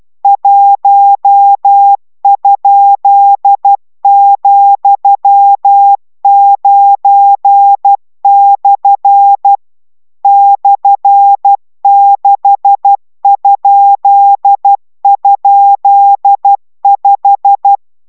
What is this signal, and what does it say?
Morse code, heard as a keyed tone, about 12 wpm, 800 Hz. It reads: 1?,9/ /6??5